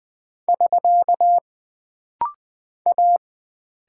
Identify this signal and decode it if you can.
Morse code, heard as a keyed tone, about 20 wpm, 700 Hz. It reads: <SK> A
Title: Single Character (Letter:Number:Punctuation:Prosign) - Rapid-Fire - 20wpm